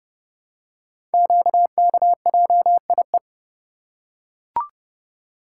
Morse code, heard as a keyed tone, about 30 wpm, 700 Hz. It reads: QKJIE
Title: Code Groups - 5 Letters - 30wpm